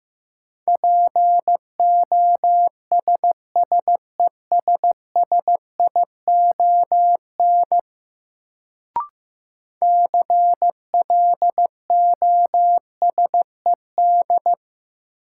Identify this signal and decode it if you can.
Morse code, heard as a keyed tone, about 15 wpm, 700 Hz. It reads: POSSESSION CLOSED
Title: Sentences from Top 1400 Words - New Word Drill - 15wpm